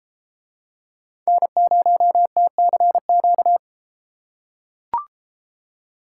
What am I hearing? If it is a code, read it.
Morse code, heard as a keyed tone, about 32 wpm, 700 Hz. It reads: N0TCQ